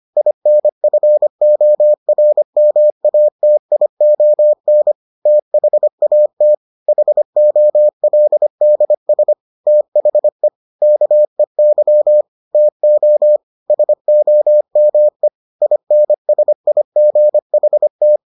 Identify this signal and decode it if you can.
Morse code, heard as a keyed tone, about 25 wpm, 600 Hz. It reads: INFORMATION THAT HOLDS THE KEY TO SOME INSIGHT